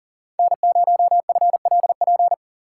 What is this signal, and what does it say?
Morse code, heard as a keyed tone, about 49 wpm, 700 Hz. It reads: N0FLP
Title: Speed Racing - USA 1x3 Call Signs 40wpm